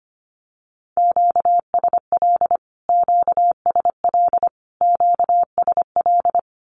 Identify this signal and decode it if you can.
Morse code, heard as a keyed tone, about 25 wpm, 700 Hz. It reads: QSL QSL QSL
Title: Q-Codes Repeated 3x 25wpm